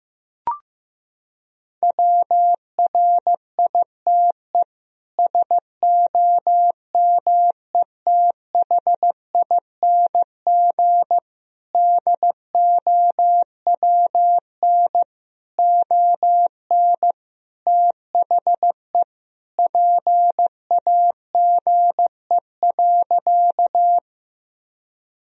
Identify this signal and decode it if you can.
Morse code, heard as a keyed tone, about 15 wpm, 700 Hz. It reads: WRITE SOMETHING DOWN ON THE PAGE.
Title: Sentences from Top 1000 Words - Review Part 1 - 15wpm